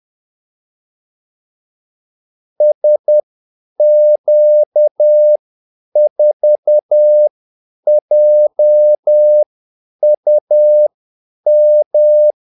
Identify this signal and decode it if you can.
Morse code, heard as a keyed tone, about 10 wpm, spaced as for 8 wpm, 600 Hz. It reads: SQ4JUM
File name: Callsigns-10-8wpm